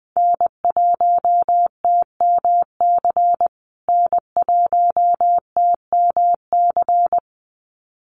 Morse code, heard as a keyed tone, about 20 wpm, 700 Hz. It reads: N1TMC N1TMC
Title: USA Call Signs Repeated 2x 20wpm